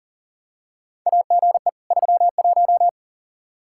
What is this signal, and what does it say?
Morse code, heard as a keyed tone, about 40 wpm, 700 Hz. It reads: AGE 31
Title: Practice QSO - Exchange 3 - More Info - 40wpm